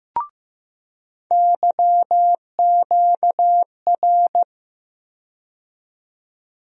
Morse code, heard as a keyed tone, about 15 wpm, 700 Hz. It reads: YQR
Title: USA Call Sign Postfixes - 3 Letters 15wpm